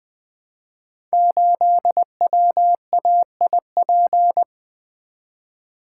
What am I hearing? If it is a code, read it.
Morse code, heard as a keyed tone, about 20 wpm, 700 Hz. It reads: 8WAIP
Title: Code Groups - 5 Letter:Numbers - 20wpm